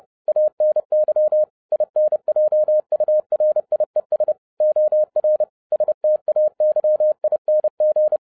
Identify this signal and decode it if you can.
Morse code, heard as a keyed tone, about 30 wpm, 610 Hz. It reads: ANY INJURIES OR STAYING
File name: 2011-07-mp3 / n55 BACK FROM A BIG ACCIDENT 30wpm